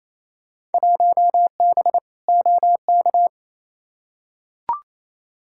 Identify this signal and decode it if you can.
Morse code, heard as a keyed tone, about 28 wpm, 700 Hz. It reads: 1B OK
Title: Field Day Exchanges Encore - 28wpm